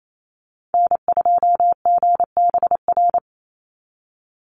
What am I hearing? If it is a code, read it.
Morse code, heard as a keyed tone, about 28 wpm, 700 Hz. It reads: N2GBR
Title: NA SOTA Activator Callsigns 28wpm